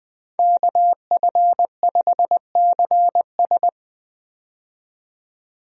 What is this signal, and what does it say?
Morse code, heard as a keyed tone, about 20 wpm, 700 Hz. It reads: KF5CS
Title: USA Call Signs - 5 Characters - 20wpm